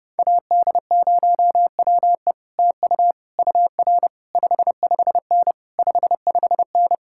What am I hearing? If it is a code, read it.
Morse code, heard as a keyed tone, about 31 wpm, 700 Hz. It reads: AD0WE TU UR 55N 55N